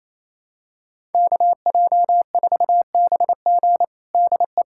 Morse code, heard as a keyed tone, about 28 wpm, 700 Hz. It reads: KJ4BG DE
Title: USA Callsign DE Callsign 28wpm